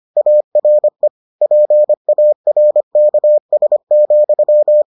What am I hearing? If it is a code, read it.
Morse code, heard as a keyed tone, about 25 wpm, 600 Hz. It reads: ARE PARKS,